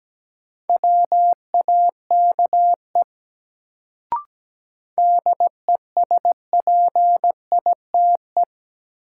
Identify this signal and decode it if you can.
Morse code, heard as a keyed tone, about 17 wpm, 700 Hz. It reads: WAKE DESPITE
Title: Sets of 4 Words - New Word Drill - 17wpm